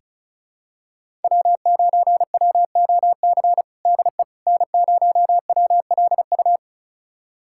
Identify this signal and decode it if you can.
Morse code, heard as a keyed tone, about 34 wpm, 700 Hz. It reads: W9WOC DE N0WLU
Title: USA Callsign DE Callsign - 5 Characters - 35wpm